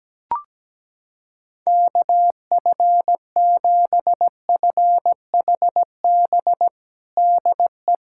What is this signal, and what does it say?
Morse code, heard as a keyed tone, about 17 wpm, 700 Hz. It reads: KF7FHB DE
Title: USA Callsign DE Callsign 17wpm